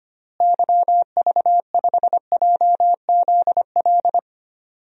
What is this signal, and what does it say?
Morse code, heard as a keyed tone, about 25 wpm, 700 Hz. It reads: YV5JZL
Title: DXCC-100 Call Signs - 6 Characters - Repeated 2x - 25wpm